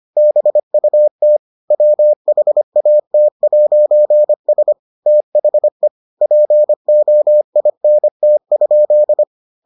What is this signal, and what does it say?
Morse code, heard as a keyed tone, about 25 wpm, 600 Hz. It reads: BUT WHAT'S THE POINT?